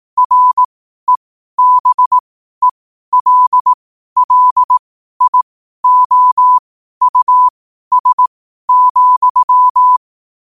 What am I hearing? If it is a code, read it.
Morse code, heard as a keyed tone, about 18 wpm, spaced as for 12 wpm, 1000 Hz. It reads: REBELLIOUS,
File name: qotdfiles / 12wpm